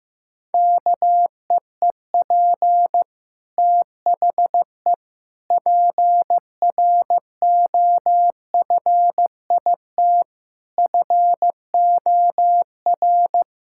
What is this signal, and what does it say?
Morse code, heard as a keyed tone, about 14 wpm, 700 Hz. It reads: KEEP THE PROFIT FOR
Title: Sentences from Top 1000 Words - Review Part 5 - 15wpm